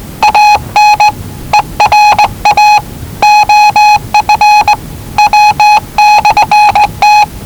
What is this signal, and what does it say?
Morse code, heard as a keyed tone, about 19 wpm, 840 Hz. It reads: AN ERA OF W/T
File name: WellingtonRadioZlwCloseDownOnHfAndMf / 01-Zlw 4277khz 417.5 qrt until 0925z